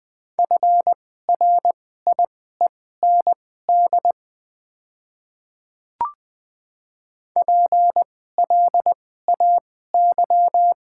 Morse code, heard as a keyed tone, about 20 wpm, spaced as for 14 wpm, 700 Hz. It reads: FRIEND PLAY
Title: 2x Character Spacing - Top 300 Words - 20wpm